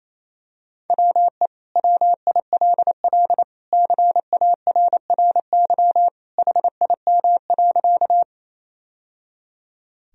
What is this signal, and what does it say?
Morse code, heard as a keyed tone, about 28 wpm, 700 Hz. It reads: WE WILL CARRY HIM.
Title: Sentences from Top 1000 Words - Review Part 1 - 28wpm